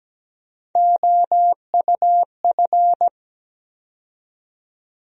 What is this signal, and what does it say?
Morse code, heard as a keyed tone, about 17 wpm, 700 Hz. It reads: OUF
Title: USA Call Sign Postfixes - 3 Letters 17wpm